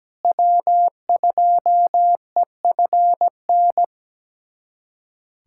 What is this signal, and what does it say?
Morse code, heard as a keyed tone, about 17 wpm, 700 Hz. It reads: W2EFN
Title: USA Callsign DE Callsign - 5 Characters - 17wpm